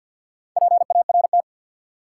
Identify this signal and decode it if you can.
Morse code, heard as a keyed tone, about 47 wpm, 700 Hz. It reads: PART
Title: Sentences from Top 200 Words - New Word Drill - 50wpm